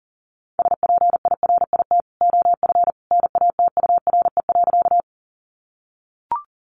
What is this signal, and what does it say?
Morse code, heard as a keyed tone, about 40 wpm, 700 Hz. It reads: SPIRIT OF NATURE.
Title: Sentences from Top 1400 Words 40wpm